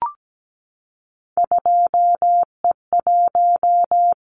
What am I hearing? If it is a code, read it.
Morse code, heard as a keyed tone, about 17 wpm, 700 Hz. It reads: 2E1
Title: Callsign Prefixes for Top 25 DXCC - Repeated 3x - 17wpm